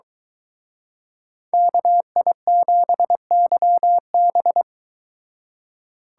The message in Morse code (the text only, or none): KI7YB